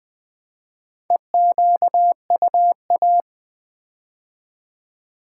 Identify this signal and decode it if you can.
Morse code, heard as a keyed tone, about 20 wpm, 700 Hz. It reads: EQUA
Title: Build A Word - 4 Letters At A Time - 8-Letter Words - 20wpm